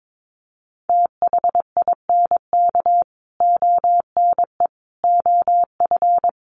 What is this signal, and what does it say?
Morse code, heard as a keyed tone, about 22 wpm, 700 Hz. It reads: THINK ONE OF
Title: Sets of 5 Words Encore2 22wpm